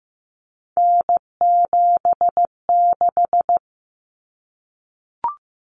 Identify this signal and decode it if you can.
Morse code, heard as a keyed tone, about 15 wpm, 700 Hz. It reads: N76
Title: Common Cut Numbers - 3-Digit 15wpm